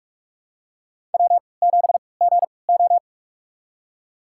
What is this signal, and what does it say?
Morse code, heard as a keyed tone, about 44 wpm, spaced as for 25 wpm, 700 Hz. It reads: W7GO
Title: USA Call Signs - 3x Character Spacing - 45wpm